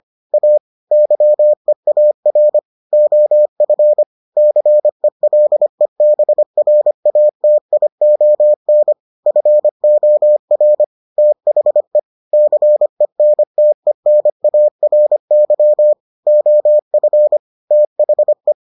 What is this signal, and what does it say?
Morse code, heard as a keyed tone, about 25 wpm, 600 Hz. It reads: A YEAR OF CELEBRATION FOR THE CENTENARY OF THE